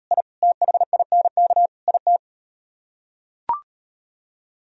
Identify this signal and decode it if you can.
Morse code, heard as a keyed tone, about 39 wpm, 700 Hz. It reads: I THINK IT